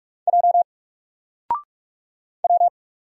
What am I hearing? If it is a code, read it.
Morse code, heard as a keyed tone, about 46 wpm, 700 Hz. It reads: J W
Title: Single Letter-Number 45wpm